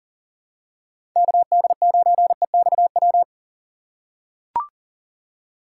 Morse code, heard as a keyed tone, about 40 wpm, 700 Hz. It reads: KD9EXW
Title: USA Call Signs - 6 Characters - 2x Character Spacing - 40wpm